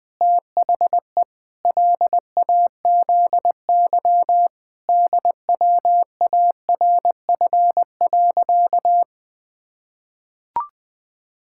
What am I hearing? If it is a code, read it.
Morse code, heard as a keyed tone, about 21 wpm, 700 Hz. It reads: THE LAZY DWARF.
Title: Pangram Challenge v2 - 20wpm